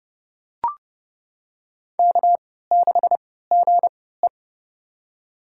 Morse code, heard as a keyed tone, about 30 wpm, spaced as for 17 wpm, 700 Hz. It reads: K6GE
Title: USA Call Signs - 4 Characters - 3x Character Spacing - 30wpm